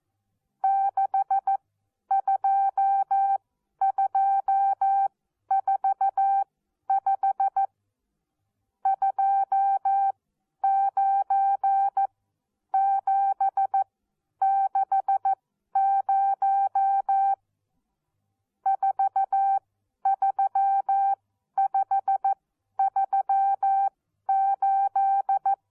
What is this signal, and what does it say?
Morse code, heard as a keyed tone, about 14 wpm, spaced as for 10 wpm, 780 Hz. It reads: 62245 29760 43538
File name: Onlinelehrgang / morsen / DJ4UF Morsen14